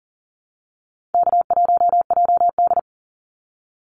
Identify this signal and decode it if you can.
Morse code, heard as a keyed tone, about 40 wpm, 700 Hz. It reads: K1JD
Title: NA SOTA Activator Callsigns 40wpm